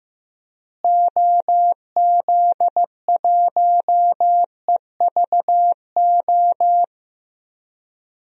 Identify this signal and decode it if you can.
Morse code, heard as a keyed tone, about 15 wpm, 700 Hz. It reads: OZ1EVO
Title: DXCC-100 Call Signs - 6 Characters - Repeated 2x - 15wpm